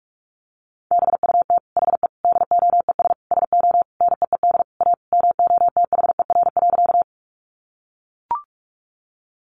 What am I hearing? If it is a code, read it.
Morse code, heard as a keyed tone, about 44 wpm, 700 Hz. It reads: BUT HE DOES SO NEED A MOTHER.
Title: Sentences from Top 200 Words 45wpm